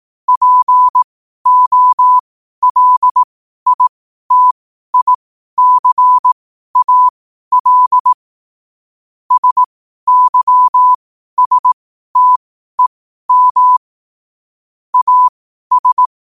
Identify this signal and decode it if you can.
Morse code, heard as a keyed tone, about 18 wpm, spaced as for 12 wpm, 1000 Hz. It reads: POLITICAL SYSTEM AS
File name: qotdfiles / 12wpm